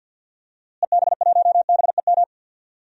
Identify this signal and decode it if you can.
Morse code, heard as a keyed tone, about 50 wpm, 700 Hz. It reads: EB1BEM
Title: DXCC-100 Call Signs - 6 Characters - Repeated 3x - 50wpm